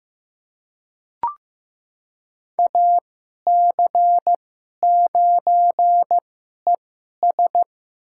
Morse code, heard as a keyed tone, about 15 wpm, spaced as for 11 wpm, 700 Hz. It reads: AC9ES
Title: USA Call Signs - 2x Character Spacing - 15wpm